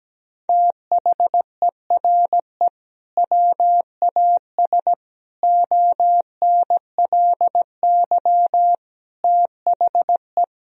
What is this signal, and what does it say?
Morse code, heard as a keyed tone, about 17 wpm, 700 Hz. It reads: THERE WAS ONLY THE
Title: Sentences from Top 1000 Words - Review Part 3 - 17wpm